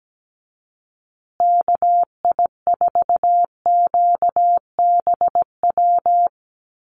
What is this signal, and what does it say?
Morse code, heard as a keyed tone, about 17 wpm, 700 Hz. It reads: KI4QBW
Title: USA Call Signs Repeated 2x 17wpm